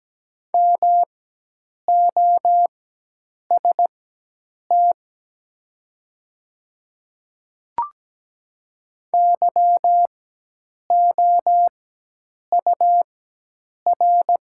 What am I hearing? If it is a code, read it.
Morse code, heard as a keyed tone, about 17 wpm, spaced as for 8 wpm, 700 Hz. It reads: MOST YOUR
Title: 4x Character Spacing - Top 100 Words - 17wpm